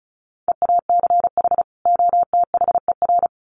Morse code, heard as a keyed tone, about 34 wpm, 700 Hz. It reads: EACH OTHER
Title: Sets of 2 Words 35wpm